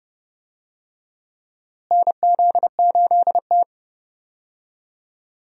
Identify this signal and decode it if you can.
Morse code, heard as a keyed tone, about 30 wpm, 700 Hz. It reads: NZ8T
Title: USA Call Signs - 4 Characters - Repeated 3x - 30wpm